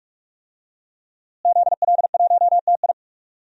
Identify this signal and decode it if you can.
Morse code, heard as a keyed tone, about 48 wpm, 700 Hz. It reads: ZL1TI
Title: DXCC-100 Call Signs - 5 Characters - Repeated 2x - 45wpm